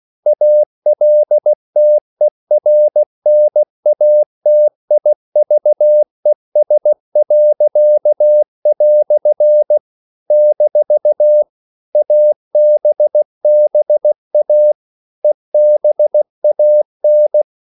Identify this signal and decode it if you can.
Morse code, heard as a keyed tone, about 16 wpm, 600 Hz. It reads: ALTERNATIVES." - ABBA EBAN
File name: CW / Practice / EnglishQuotations0416